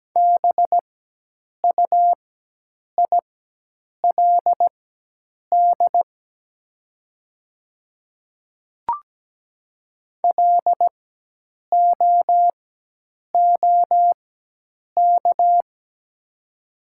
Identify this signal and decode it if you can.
Morse code, heard as a keyed tone, about 17 wpm, spaced as for 8 wpm, 700 Hz. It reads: BUILD LOOK
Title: 4x Character Spacing - Top 200 Words - 17wpm